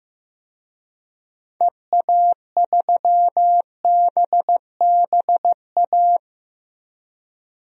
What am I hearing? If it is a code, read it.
Morse code, heard as a keyed tone, about 15 wpm, 700 Hz. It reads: EA3BBA
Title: DXCC-100 Call Signs - 6 Characters - 15wpm